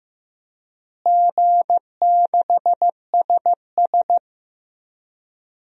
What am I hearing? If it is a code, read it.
Morse code, heard as a keyed tone, about 15 wpm, 700 Hz. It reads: G6SS
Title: DXCC-100 Call Signs - 4 Characters - Repeated 3x - 15wpm